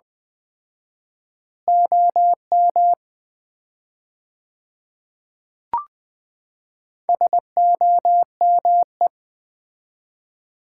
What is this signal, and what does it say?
Morse code, heard as a keyed tone, about 20 wpm, 700 Hz. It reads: OM SOME
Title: Common QSO Phrases - 4x Word-Spacing - 20wpm